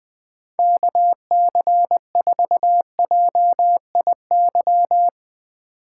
Morse code, heard as a keyed tone, about 20 wpm, 700 Hz. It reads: KC4JIY
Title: USA Call Signs - 6 Characters - 6x Character Spacing - 20wpm